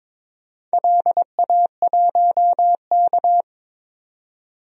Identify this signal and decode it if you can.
Morse code, heard as a keyed tone, about 21 wpm, 700 Hz. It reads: LA1K
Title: DXCC-100 Call Signs - 4 Characters - 22wpm